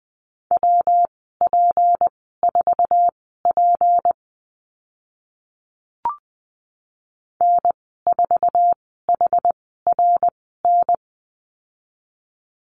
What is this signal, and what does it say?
Morse code, heard as a keyed tone, about 20 wpm, spaced as for 15 wpm, 700 Hz. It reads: WP4P N4HRN
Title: 173 Lesson 28g - Number 9 - Call Signs 20wpm